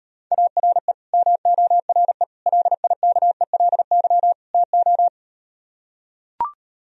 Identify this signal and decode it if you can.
Morse code, heard as a keyed tone, about 38 wpm, 700 Hz. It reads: ARE MORE LIKELY TO